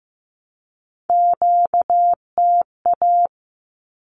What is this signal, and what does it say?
Morse code, heard as a keyed tone, about 15 wpm, 700 Hz. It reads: QTA
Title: Q-Codes 15wpm